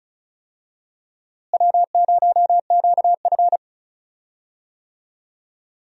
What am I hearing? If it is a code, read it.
Morse code, heard as a keyed tone, about 34 wpm, 700 Hz. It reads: W0QF
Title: USA Call Signs - 4 Characters - Repeated 2x - 35wpm